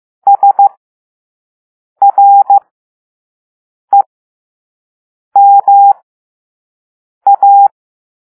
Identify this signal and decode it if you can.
Morse code, heard as a keyed tone, about 15 wpm, spaced as for 5 wpm, 800 Hz. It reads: SREMA